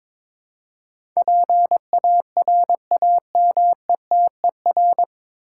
Morse code, heard as a keyed tone, about 22 wpm, 700 Hz. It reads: PARAMETER
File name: practice / 22wpm / Nine Letter Words - 22wpm